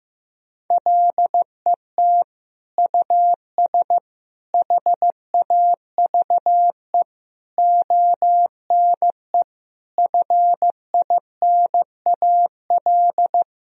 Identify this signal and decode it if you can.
Morse code, heard as a keyed tone, about 16 wpm, 700 Hz. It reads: LET US HAVE ONE FINAL
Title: Sentences from Top 1000 Words - Review Part 2 - 15wpm